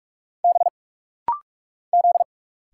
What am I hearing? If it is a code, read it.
Morse code, heard as a keyed tone, about 46 wpm, 690 Hz. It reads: B Z
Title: Single Letters - Rapid-Fire 45wpm